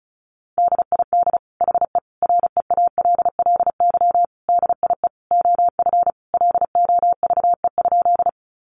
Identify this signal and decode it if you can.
Morse code, heard as a keyed tone, about 34 wpm, 700 Hz. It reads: DID HE REALLY DIE OF LOVE?
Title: Sentences from Top 1300 Words 35wpm